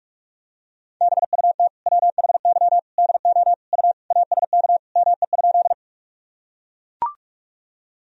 Morse code, heard as a keyed tone, about 43 wpm, 700 Hz. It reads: BUT WHY DO U ASK ME?
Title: Sentences from Top 200 Words and Abbreviations 45wpm